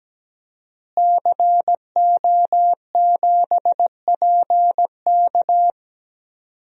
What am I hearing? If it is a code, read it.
Morse code, heard as a keyed tone, about 17 wpm, 700 Hz. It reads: CO7PK
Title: DXCC-100 Call Signs - 5 Characters - Repeated 2x - 17wpm